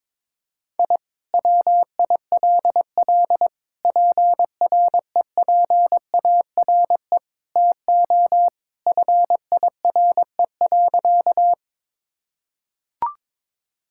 Morse code, heard as a keyed tone, about 22 wpm, 700 Hz. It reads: I WILL PREPARE TO FIRE.
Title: Sentences from Top 1000 Words - Review Part 6 - 22wpm